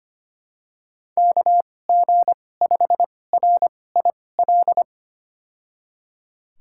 Morse code, heard as a keyed tone, about 25 wpm, spaced as for 18 wpm, 700 Hz. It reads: KG5RIL
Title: USA Call Signs - 6 Characters - 2x Character Spacing - 25wpm